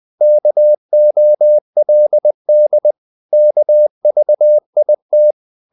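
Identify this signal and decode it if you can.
Morse code, heard as a keyed tone, about 20 wpm, 600 Hz. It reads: KOLD KVIT
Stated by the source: Morse code practice of short Danish words at 20 wpm, from CW med Gnister.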